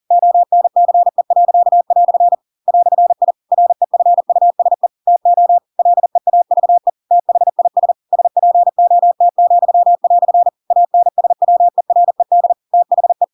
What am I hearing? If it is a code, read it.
Morse code, heard as a keyed tone, about 40 wpm, 700 Hz. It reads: ONCE." "I REFUSE TO LEAVE THIS SPOT," ANSWERED THE
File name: Pinocchio / Pinocchio ch4 40 40 ik1hge